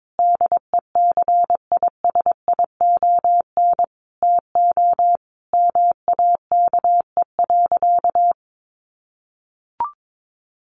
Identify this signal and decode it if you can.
Morse code, heard as a keyed tone, about 22 wpm, 700 Hz. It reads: DECISION TO MAKE.
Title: Sentences from Top 1000 Words 22wpm